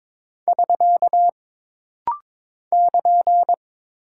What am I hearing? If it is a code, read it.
Morse code, heard as a keyed tone, about 22 wpm, 700 Hz. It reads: <SK> <KN>
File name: practice / 22wpm / Single Punctuation-Prosign - Rapid-Fire - 22wpm